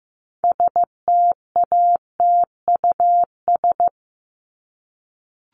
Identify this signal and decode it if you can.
Morse code, heard as a keyed tone, about 15 wpm, 700 Hz. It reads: STATUS
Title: Top 1251-1500 Words 15wpm